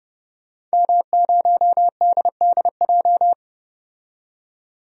Morse code, heard as a keyed tone, about 32 wpm, 700 Hz. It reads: M0DDJ
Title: DXCC-100 Call Signs - 5 Characters - 30wpm